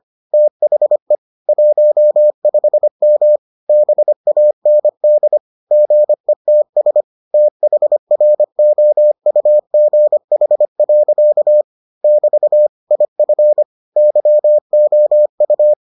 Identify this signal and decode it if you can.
Morse code, heard as a keyed tone, about 25 wpm, 600 Hz. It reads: THE 15M BAND GETS THROUGH. <BT> IF YOU